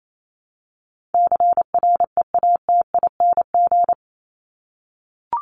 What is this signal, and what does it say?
Morse code, heard as a keyed tone, about 28 wpm, 700 Hz. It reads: CREATING
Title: ING Suffix - Top 100 Words with Base Words 28wpm